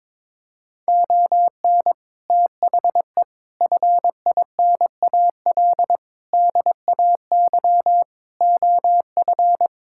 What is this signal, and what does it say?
Morse code, heard as a keyed tone, about 22 wpm, 700 Hz. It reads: ON THE FINAL DAY OF
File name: practice / 22wpm / Sets of 6 Words - 22wpm